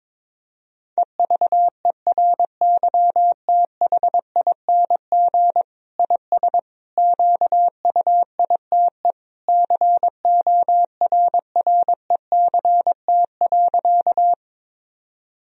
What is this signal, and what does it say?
Morse code, heard as a keyed tone, about 22 wpm, 700 Hz. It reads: EVERYTHING IS QUITE CORRECT.
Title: Sentences from Top 1000 Words - Review Part 2 - 22wpm